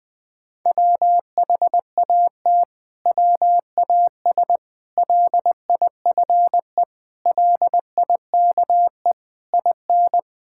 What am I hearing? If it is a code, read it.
Morse code, heard as a keyed tone, about 21 wpm, 700 Hz. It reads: WHAT WAS LIFE LIKE IN
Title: Sentences from Top 1000 Words - Review Part 5 - 20wpm